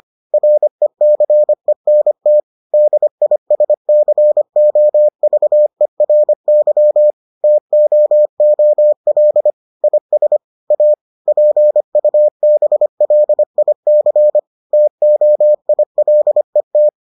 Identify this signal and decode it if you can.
Morse code, heard as a keyed tone, about 25 wpm, 600 Hz. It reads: RECENT DISCOVERY TOOL IS A PUBLIC TOILET